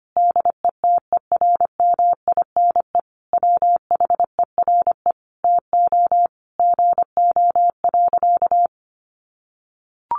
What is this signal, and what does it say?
Morse code, heard as a keyed tone, about 25 wpm, 700 Hz. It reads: DETERMINE WHERE TO GO.
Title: Sentences from Top 700 Words 25wpm